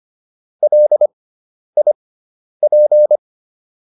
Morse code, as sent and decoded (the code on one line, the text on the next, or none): .-.. .. .--.
LIP